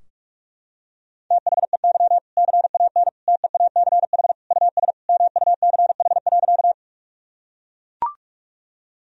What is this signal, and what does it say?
Morse code, heard as a keyed tone, about 41 wpm, 700 Hz. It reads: THEY CAN TEACH US MUCH.
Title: Sentences from Top 1000 Words - Review Part 2 - 45wpm